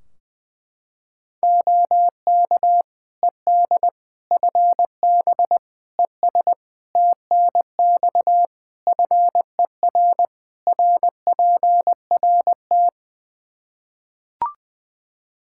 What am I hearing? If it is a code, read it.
Morse code, heard as a keyed tone, about 21 wpm, 700 Hz. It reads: OK ED FB ES TNX FER RPRT
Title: K1USN SST QSO Practice 20wpm